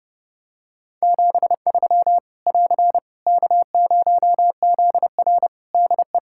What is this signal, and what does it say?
Morse code, heard as a keyed tone, about 30 wpm, 700 Hz. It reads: 73 <AR> K0ZR DE
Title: Practice QSO - Exchange 4 - Ending - 30wpm